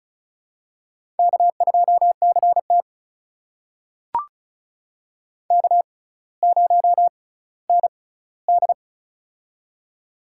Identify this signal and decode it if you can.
Morse code, heard as a keyed tone, about 36 wpm, spaced as for 18 wpm, 700 Hz. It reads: K2CT K0ND
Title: USA Call Signs - 4 Characters - 6x Character Spacing - 35wpm